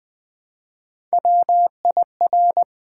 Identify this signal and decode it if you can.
Morse code, heard as a keyed tone, about 20 wpm, 700 Hz. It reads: WIR
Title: Build A Word - 1 Letter At A Time - 4-Letter Words - 20wpm